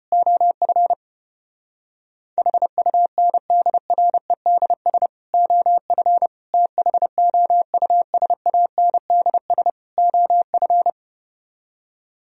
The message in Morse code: OF HUNDREDS OF THOUSANDS OF

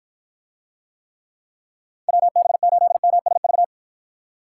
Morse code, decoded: WB8MSV

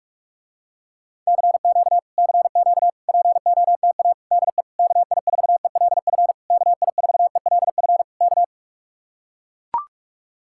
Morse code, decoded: CQ CQ POTA DE KI4ELF KI4ELF K

